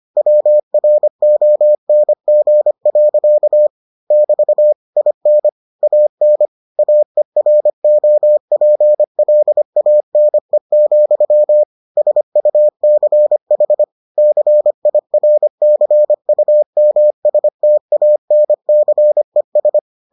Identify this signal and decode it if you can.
Morse code, heard as a keyed tone, about 25 wpm, 600 Hz. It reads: WRONG. <BT> IN AN AEROPLANE, SUCH CIRCUMSTANCES